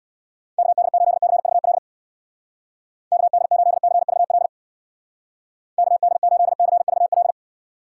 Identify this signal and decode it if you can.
Morse code, heard as a keyed tone, about 63 wpm, 700 Hz. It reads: KD8CVL KD8CVL KD8CVL